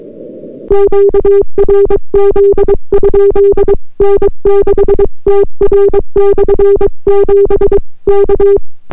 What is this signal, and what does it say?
Morse code, heard as a keyed tone, about 22 wpm, 400 Hz. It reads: QRZ? N6TR/7 K